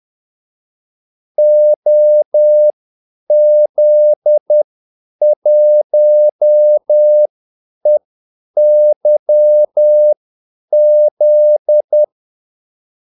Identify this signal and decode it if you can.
Morse code, heard as a keyed tone, about 10 wpm, spaced as for 8 wpm, 600 Hz. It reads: OZ1EYZ
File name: Callsigns-10-8wpm